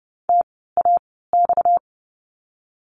Morse code, heard as a keyed tone, about 30 wpm, spaced as for 17 wpm, 700 Hz. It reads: TAX
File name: course / 30:10wpm / 230 Lesson 37f - Letter x - Words 30wpm